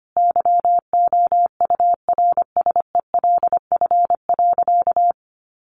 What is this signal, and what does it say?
Morse code, heard as a keyed tone, about 25 wpm, 700 Hz. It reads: YOURSELF.
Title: Sentences from Top 600 Words 25wpm